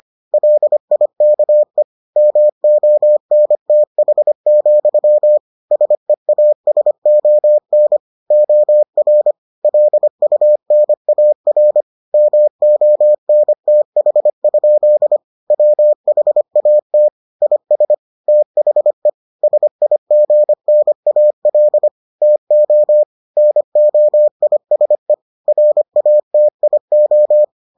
Morse code, heard as a keyed tone, about 25 wpm, 600 Hz. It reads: LIKE MONTH, SEASON OR LUNAR MONTH? WHAT IS THE SIGNAL TO NOISE RATIO